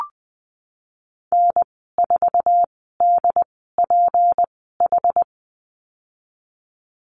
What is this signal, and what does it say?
Morse code, heard as a keyed tone, about 20 wpm, spaced as for 15 wpm, 700 Hz. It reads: N4DPH
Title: 167 Lesson 27g - Number 7 - Call Signs 20wpm